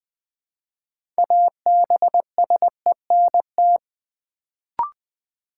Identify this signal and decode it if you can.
Morse code, heard as a keyed tone, about 20 wpm, 700 Hz. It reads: ABSENT
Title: Build A Word - 2 Letters At A Time - 6-Letter Words - 20wpm